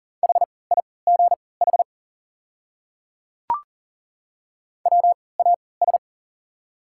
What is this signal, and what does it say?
Morse code, heard as a keyed tone, about 39 wpm, spaced as for 22 wpm, 690 Hz. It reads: HIGH WAS
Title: 3x Character Spacing - Top 200 Words - 40wpm